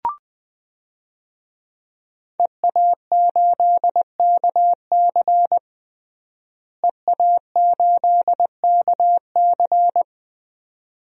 Morse code, heard as a keyed tone, about 20 wpm, 700 Hz. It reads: EA8KC EA8KC
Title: DXCC-100 Call Signs - 5 Characters - Repeated 2x - 20wpm